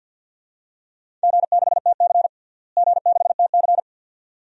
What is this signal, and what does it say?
Morse code, heard as a keyed tone, about 48 wpm, 700 Hz. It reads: G6TC G6TC
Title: DXCC-100 Call Signs - 4 Characters - Repeated 2x - 50wpm